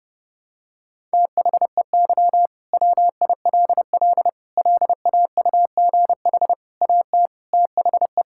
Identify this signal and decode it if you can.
Morse code, heard as a keyed tone, about 30 wpm, 700 Hz. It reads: THEY WILL LAUGH AT THE
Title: Sentences from Top 1000 Words - Review Part 6 - 30wpm